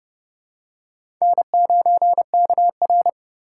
Morse code, heard as a keyed tone, about 31 wpm, 700 Hz. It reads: N9KR
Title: K1USN SST QSO Practice - 30wpm